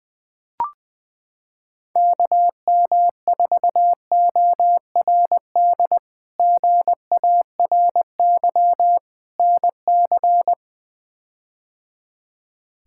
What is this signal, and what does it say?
Morse code, heard as a keyed tone, about 20 wpm, 700 Hz. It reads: KM4ORD GARY NC
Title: K1USN SST QSO Practice - 20wpm